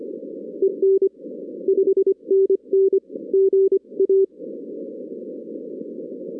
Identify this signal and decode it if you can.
Morse code, heard as a keyed tone, about 25 wpm, 390 Hz. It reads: EN 5NN GA
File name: 2016 arrl dx cw / 2016-ARRL-DX-CW-K5TR- - 0-22